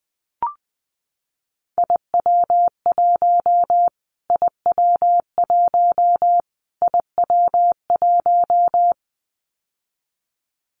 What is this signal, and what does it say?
Morse code, heard as a keyed tone, about 20 wpm, 700 Hz. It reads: IW1 IW1 IW1
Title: Callsign Prefixes for Top 25 DXCC - Repeated 3x - 20wpm